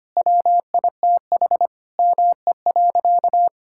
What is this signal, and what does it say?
Morse code, heard as a keyed tone, about 25 wpm, 700 Hz. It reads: WITH ME.
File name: practice / 25wpm / Sentences with 4-Word Phrases - 25wpm